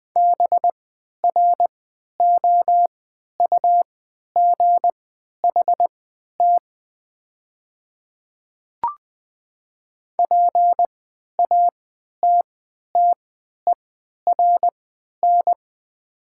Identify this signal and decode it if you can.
Morse code, heard as a keyed tone, about 20 wpm, spaced as for 11 wpm, 700 Hz. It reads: BROUGHT PATTERN
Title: 3x Character Spacing - Top 500 Words - 20wpm